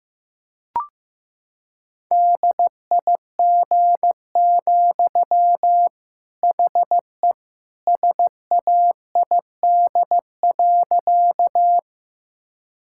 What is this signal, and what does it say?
Morse code, heard as a keyed tone, about 15 wpm, 700 Hz. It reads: DIG, HE SAID.
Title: Sentences from Top 1600 Words 15wpm